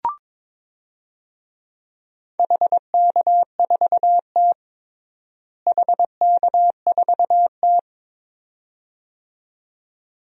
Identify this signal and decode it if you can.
Morse code, heard as a keyed tone, about 22 wpm, 700 Hz. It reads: HK4T HK4T
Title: DXCC-100 Call Signs - 4 Characters - Repeated 2x - 22wpm